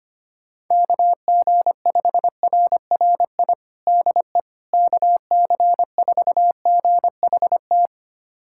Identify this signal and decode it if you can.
Morse code, heard as a keyed tone, about 25 wpm, 700 Hz. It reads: KG5RRI DE KC4GHT
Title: USA Callsign DE Callsign - 6 Characters - 25wpm